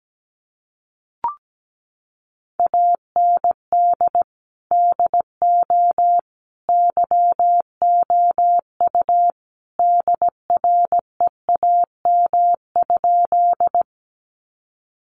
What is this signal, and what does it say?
Morse code, heard as a keyed tone, about 17 wpm, 700 Hz. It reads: AND DO YOU DREAM?
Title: Sentences from Top 1500 Words 17wpm